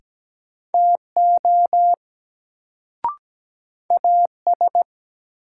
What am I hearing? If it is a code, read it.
Morse code, heard as a keyed tone, about 17 wpm, 700 Hz. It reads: TO AS
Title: Sets of 2 Words - New Word Drill - 17wpm